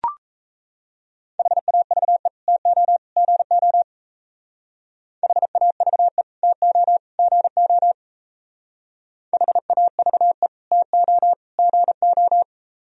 Morse code, heard as a keyed tone, about 35 wpm, 700 Hz. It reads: HAVE TO GO HAVE TO GO HAVE TO GO